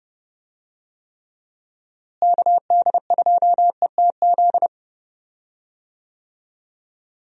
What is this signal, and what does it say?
Morse code, heard as a keyed tone, about 30 wpm, 700 Hz. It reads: KD2ETZ
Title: USA Call Signs - 6 Characters - Repeated 3x - 30wpm